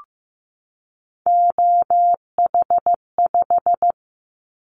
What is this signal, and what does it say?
Morse code, heard as a keyed tone, about 15 wpm, 700 Hz. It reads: OH5
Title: Callsign Prefixes for Top 25 DXCC - 15wpm